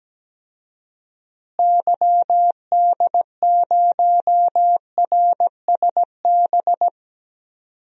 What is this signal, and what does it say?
Morse code, heard as a keyed tone, about 17 wpm, 700 Hz. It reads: YD0RSB
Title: DXCC-100 Call Signs - 6 Characters - Repeated 3x - 17wpm